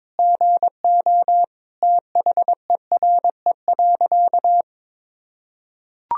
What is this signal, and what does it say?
Morse code, heard as a keyed tone, about 22 wpm, 700 Hz. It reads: GO THERE.
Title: Sentences from Top 1000 Words - Review Part 1 - 22wpm